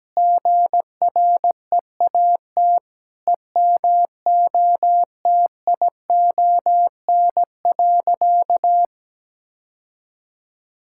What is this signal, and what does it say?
Morse code, heard as a keyed tone, about 17 wpm, 700 Hz. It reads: GREAT EMOTION.
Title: Sentences from Top 1600 Words 17wpm